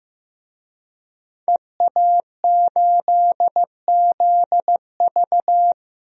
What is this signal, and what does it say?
Morse code, heard as a keyed tone, about 15 wpm, 700 Hz. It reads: EA8ZV
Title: DXCC-100 Call Signs - 5 Characters - 15wpm